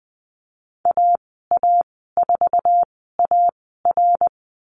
Morse code, heard as a keyed tone, about 20 wpm, spaced as for 14 wpm, 700 Hz. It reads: AA4AR
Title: 082 Lesson 14g - Number 2 - Call Signs 20wpm